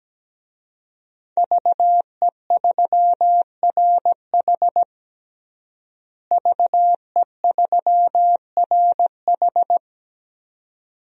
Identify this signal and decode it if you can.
Morse code, heard as a keyed tone, about 17 wpm, 700 Hz. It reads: VE3RH VE3RH
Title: DXCC-100 Call Signs - 5 Characters - Repeated 2x - 17wpm